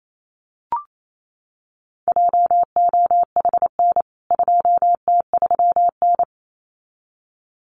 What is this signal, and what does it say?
Morse code, heard as a keyed tone, about 28 wpm, 700 Hz. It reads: JOHN 2T3N
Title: CWT Exchange 28wpm